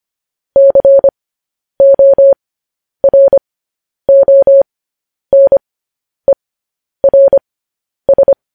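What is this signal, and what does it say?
Morse code, heard as a keyed tone, about 25 wpm, spaced as for 10 wpm, 550 Hz. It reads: CORONERS
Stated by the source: Morse code news headlines at 10.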